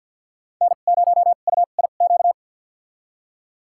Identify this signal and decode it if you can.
Morse code, heard as a keyed tone, about 53 wpm, spaced as for 37 wpm, 700 Hz. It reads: N0UIQ
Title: USA Call Signs - 5 Characters - 2x Character Spacing - 50wpm